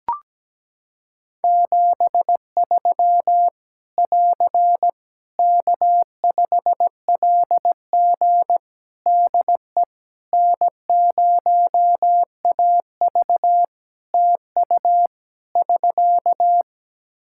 Morse code, heard as a keyed tone, about 17 wpm, 700 Hz. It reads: 73 <AR> K5LG DE N0AV TU <SK>
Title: K1USN SST QSO Practice 17wpm